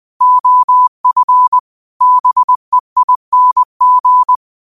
Morse code, heard as a keyed tone, about 20 wpm, 1000 Hz. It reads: OF BEING
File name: qotdfiles / 20wpm